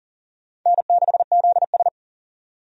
Morse code, heard as a keyed tone, about 38 wpm, 700 Hz. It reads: N6ZS
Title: USA Call Signs - 4 Characters - 40wpm